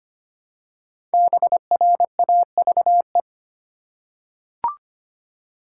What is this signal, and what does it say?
Morse code, heard as a keyed tone, about 25 wpm, 700 Hz. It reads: BRAVE
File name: 5 Letter Words - First 3 Letters Given - 25wpm